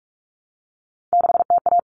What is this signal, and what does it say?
Morse code, heard as a keyed tone, about 44 wpm, 700 Hz. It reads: 6TA
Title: All Cut Numbers - 3-Digit 45wpm